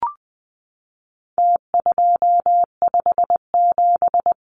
Morse code, heard as a keyed tone, about 20 wpm, 700 Hz. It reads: T257
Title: Numbers - Contest Style - 4-Digit 20wpm